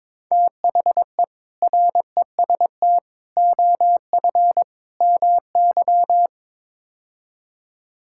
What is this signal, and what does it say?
Morse code, heard as a keyed tone, about 22 wpm, 700 Hz. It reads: THE REST OF MY
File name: practice / 22wpm / Build A Phrase - 5-Word Phrases - 22wpm